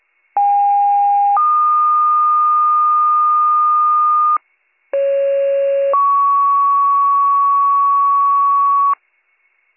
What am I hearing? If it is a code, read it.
Two-tone page, heard as a one-second tone, then a three-second tone, about 810 then 1220 Hz, about 550 then 1120 Hz. It is heard from 0.4 to 8.9 s.